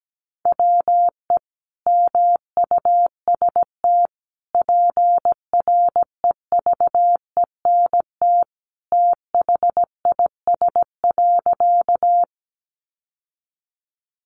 Morse code, heard as a keyed tone, about 17 wpm, 700 Hz. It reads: WE MUST PREVENT THIS.
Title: Sentences from Top 1100 Words 17wpm